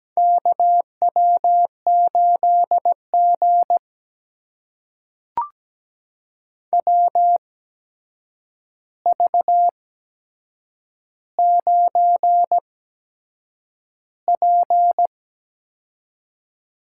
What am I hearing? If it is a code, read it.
Morse code, heard as a keyed tone, about 17 wpm, spaced as for 7 wpm, 700 Hz. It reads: KW8G WV9P
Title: USA Call Signs - 4 Characters - 8x Character Spacing - 17wpm